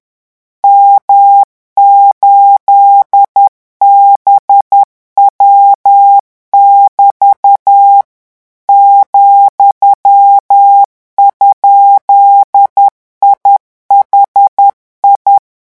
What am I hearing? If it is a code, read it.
Morse code, heard as a keyed tone, about 10 wpm, 800 Hz. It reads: M8BW<BT> ,?IHI
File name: morse mp3 / revision17-vitesse 10 mots